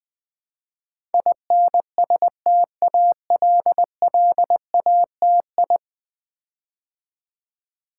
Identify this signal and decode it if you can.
Morse code, heard as a keyed tone, about 20 wpm, 700 Hz. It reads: INSTALLATI
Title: Build A Word - 5 Letters At A Time - 12-Letter Words - 20wpm